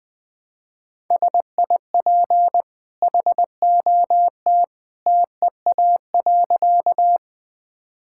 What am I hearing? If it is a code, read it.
Morse code, heard as a keyed tone, about 21 wpm, 700 Hz. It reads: SIP HOT TEA.
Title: Easy Sentences with Hints - 20wpm